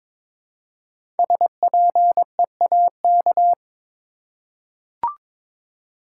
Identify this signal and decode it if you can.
Morse code, heard as a keyed tone, about 22 wpm, 700 Hz. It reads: SPEAK
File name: 5 Letter Words - First 3 Letters Given - 22wpm